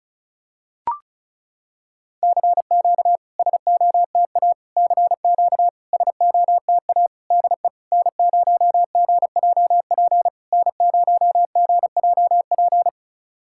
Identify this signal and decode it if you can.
Morse code, heard as a keyed tone, about 36 wpm, 690 Hz. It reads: CQ SOTA CQ SOTA DE N0GJP N0GJP
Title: SOTA QSO Practice 35wpm